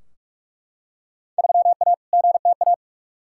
Morse code, heard as a keyed tone, about 42 wpm, 700 Hz. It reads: 3A GTA
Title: Field Day Exchanges - 45wpm